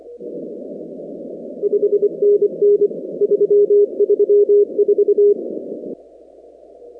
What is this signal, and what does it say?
Morse code, heard as a keyed tone, about 25 wpm, 410 Hz. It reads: 5NN 334
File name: contests / 2019-cw-wpx-kt5j-k5pi- -2330 0-19